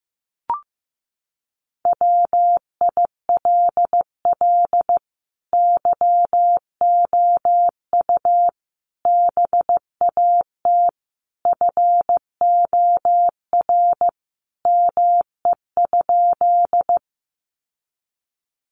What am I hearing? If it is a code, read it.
Morse code, heard as a keyed tone, about 15 wpm, 700 Hz. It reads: WILL YOU BAT FOR ME?
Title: Sentences from Top 1500 Words 15wpm